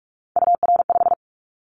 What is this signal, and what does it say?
Morse code, heard as a keyed tone, about 46 wpm, 700 Hz. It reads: UR5
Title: Callsign Prefixes for Top 25 DXCC - 45wpm